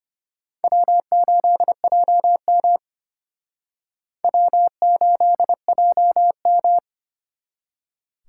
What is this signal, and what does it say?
Morse code, heard as a keyed tone, about 31 wpm, 700 Hz. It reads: W8JM W8JM
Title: Speed Racing - USA 1x2 Call Signs 25wpm